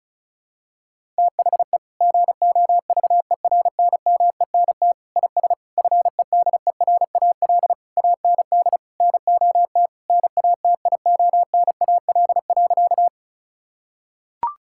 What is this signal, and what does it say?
Morse code, heard as a keyed tone, about 36 wpm, 700 Hz. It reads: THE GOVERNMENT IS FEDERAL AND NOT NATIONAL.
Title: Sentences from Top 1000 Words - Review Part 5 - 35wpm